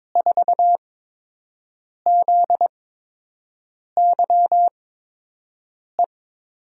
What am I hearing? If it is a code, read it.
Morse code, heard as a keyed tone, about 22 wpm, spaced as for 6 wpm, 700 Hz. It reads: 4ZYE
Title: USA Call Signs - 6 Characters - 8x Character Spacing - 22wpm